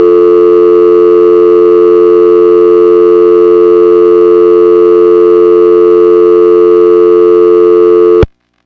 Dial tone (North America), 350 + 440 Hz.